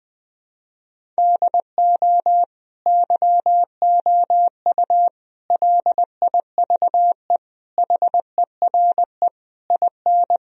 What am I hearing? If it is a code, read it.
Morse code, heard as a keyed tone, about 21 wpm, 700 Hz. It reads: DO YOU LIVE HERE IN
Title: Sentences from Top 1000 Words - Review Part 1 - 20wpm